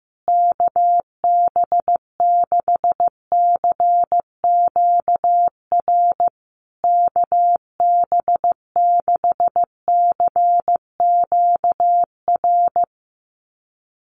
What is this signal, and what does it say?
Morse code, heard as a keyed tone, about 14 wpm, 700 Hz. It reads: KB6CQR KB6CQR
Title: USA Call Signs Repeated 3x 15wpm